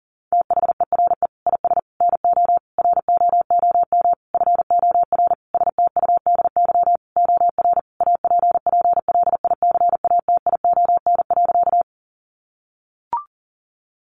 Morse code, heard as a keyed tone, about 40 wpm, 700 Hz. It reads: THERE IS NO ROOM FOR STUDY OR APPLICATION.
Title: Sentences from Top 1200 Words 40wpm